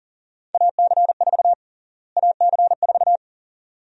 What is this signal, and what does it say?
Morse code, heard as a keyed tone, about 40 wpm, 700 Hz. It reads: AC4 AC4
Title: USA Call Sign Prefixes - Repeated 2x - 40wpm